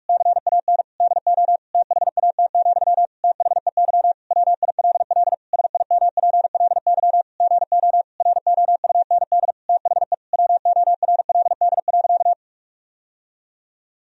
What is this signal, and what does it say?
Morse code, heard as a keyed tone, about 46 wpm, 700 Hz. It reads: CAN DO THAT, THEY WILL SIMPLY GO ROUND THE WORLD.
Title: Sentences from Top 1000 Words - Review Part 4 - 45wpm